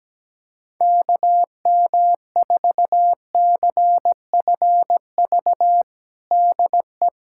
Morse code, heard as a keyed tone, about 17 wpm, 700 Hz. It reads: KM4CFV DE
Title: USA Callsign DE Callsign - 6 Characters - 17wpm